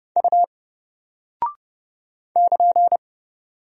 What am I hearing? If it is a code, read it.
Morse code, heard as a keyed tone, about 30 wpm, 700 Hz. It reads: U <KN>
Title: Single Character (Letter:Number:Punctuation:Prosign) Alt 30wpm